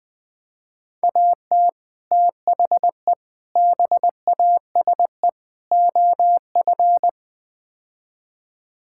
Morse code, heard as a keyed tone, about 20 wpm, 700 Hz. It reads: AT THE BASE OF
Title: Build A Phrase - 5-Word Phrases - 20wpm